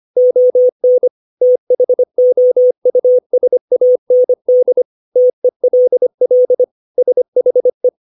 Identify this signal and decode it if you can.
Morse code, heard as a keyed tone, about 25 wpm, 500 Hz. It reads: ON THOUSAND TELL SHE
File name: 632 words / 632 Words 25WPM